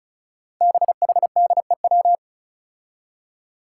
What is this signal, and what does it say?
Morse code, heard as a keyed tone, about 34 wpm, 700 Hz. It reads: BHDEW
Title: Five Challenging Letter-Number Combinations 35wpm